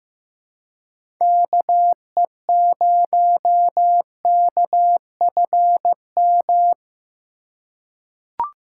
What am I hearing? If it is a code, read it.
Morse code, heard as a keyed tone, about 15 wpm, 700 Hz. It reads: KE0KFM
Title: USA Call Signs - 6 Characters - 3x Character Spacing - 15wpm